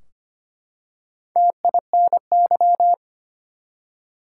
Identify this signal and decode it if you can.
Morse code, heard as a keyed tone, about 25 wpm, 700 Hz. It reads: TINY
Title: Four Letter Words - 25wpm